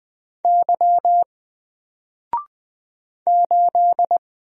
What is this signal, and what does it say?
Morse code, heard as a keyed tone, about 20 wpm, 700 Hz. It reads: Y 8
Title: Single Character (Letter:Number:Punctuation:Prosign) Alt 20wpm